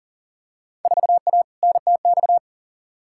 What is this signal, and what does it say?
Morse code, heard as a keyed tone, about 40 wpm, 700 Hz. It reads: 4A NTX
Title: Field Day Exchanges - 40wpm